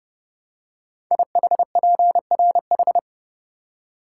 Morse code, heard as a keyed tone, about 30 wpm, 700 Hz. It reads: IHPRH
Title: Code Groups - 5 Letters - 30wpm